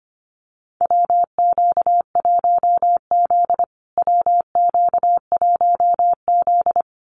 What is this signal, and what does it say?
Morse code, heard as a keyed tone, about 25 wpm, 700 Hz. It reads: WQ1Z WQ1Z
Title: USA Call Signs Repeated 2x 25wpm